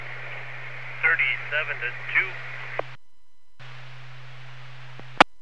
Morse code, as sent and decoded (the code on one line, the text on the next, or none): none